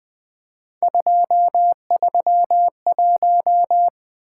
Morse code, heard as a keyed tone, about 20 wpm, 700 Hz. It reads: 231
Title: Three Letter-Number Combinations - 20wpm